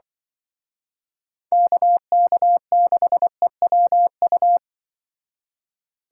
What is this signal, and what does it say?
Morse code, heard as a keyed tone, about 24 wpm, 700 Hz. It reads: KK6EWU
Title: Speed Racing - USA 2x3 Call Signs 20wpm